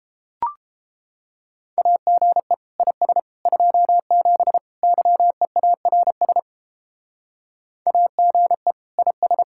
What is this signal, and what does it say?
Morse code, heard as a keyed tone, about 32 wpm, 700 Hz. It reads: AGE IS 27 YEARS AGE IS